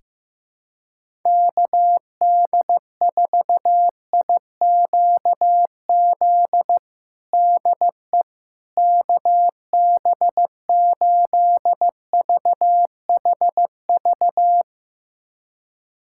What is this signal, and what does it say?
Morse code, heard as a keyed tone, about 15 wpm, 700 Hz. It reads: KD4IQZ DE KB8VHV
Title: USA Callsign DE Callsign - 6 Characters - 15wpm